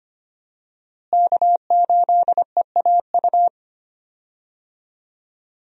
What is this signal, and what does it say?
Morse code, heard as a keyed tone, about 25 wpm, 700 Hz. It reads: K8EAU
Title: USA Call Signs - 5 Characters - Repeated 2x - 25wpm